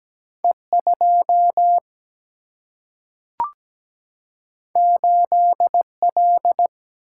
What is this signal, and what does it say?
Morse code, heard as a keyed tone, about 17 wpm, 700 Hz. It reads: E 2 8 L
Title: Single-Letter Single-Number 17wpm